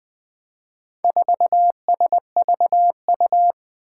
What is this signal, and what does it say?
Morse code, heard as a keyed tone, about 21 wpm, 700 Hz. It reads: 4SVU
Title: Four Challenging Letter-Number Combinations 20wpm